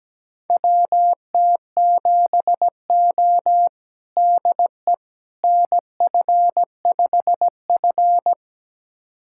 Morse code, heard as a keyed tone, about 17 wpm, 700 Hz. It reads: WT7O DE NF5F
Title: USA Callsign DE Callsign - 4 Characters - 17wpm